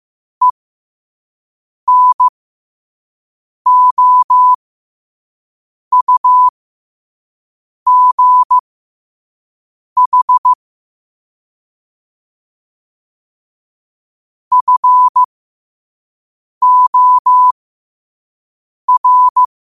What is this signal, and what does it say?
Morse code, heard as a keyed tone, about 15 wpm, spaced as for 5 wpm, 1000 Hz. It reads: ENOUGH FOR
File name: qotdfiles / 5wpm